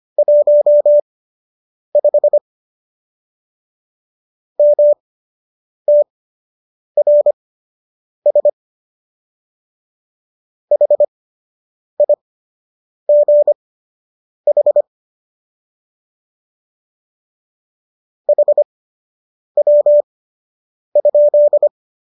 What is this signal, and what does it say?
Morse code, heard as a keyed tone, about 25 wpm, spaced as for 8 wpm, 590 Hz. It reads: 15 MTRS HIGH HW?